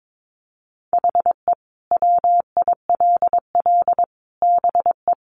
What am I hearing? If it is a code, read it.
Morse code, heard as a keyed tone, about 22 wpm, 700 Hz. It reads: HE WILL BE
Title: Sentences from Top 800 Words 22wpm